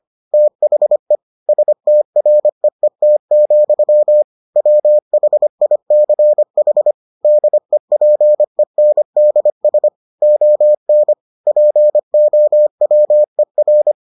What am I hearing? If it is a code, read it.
Morse code, heard as a keyed tone, about 25 wpm, 600 Hz. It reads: THE STREET, WHICH DEPENDS ON POWER